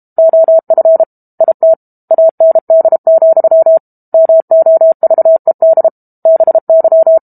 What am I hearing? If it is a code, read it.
Morse code, heard as a keyed tone, about 32 wpm, 650 Hz. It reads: OF IT AND, MOVED BY
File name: Book Two / 06-Chapter 6 - 32 WPM